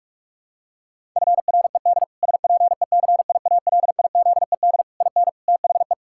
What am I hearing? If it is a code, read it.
Morse code, heard as a keyed tone, about 46 wpm, 700 Hz. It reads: FRED SPECIALIZED IN THE
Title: Pangram Challenge v2 - 45wpm